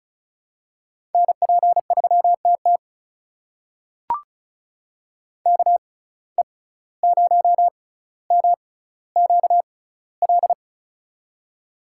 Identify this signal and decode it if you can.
Morse code, heard as a keyed tone, about 36 wpm, spaced as for 17 wpm, 700 Hz. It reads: NP3TT KE0MQL
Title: USA Call Signs - 6x Character Spacing - 35wpm